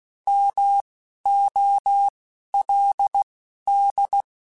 Morse code, heard as a keyed tone, about 16 wpm, spaced as for 11 wpm, 800 Hz. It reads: MOLD
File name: Alphabetical / Alphabetical mp3s / Ms @10